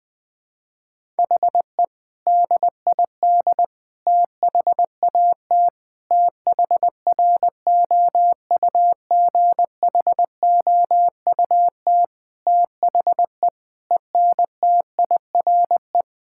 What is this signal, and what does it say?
Morse code, heard as a keyed tone, about 21 wpm, 700 Hz. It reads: HE DID THAT THROUGHOUT THE ENTIRE
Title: Sentences from Top 1000 Words - Review Part 5 - 20wpm